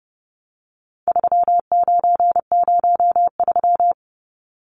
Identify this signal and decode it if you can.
Morse code, heard as a keyed tone, about 30 wpm, 700 Hz. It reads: 3903
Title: Numbers - 4-Digit 30wpm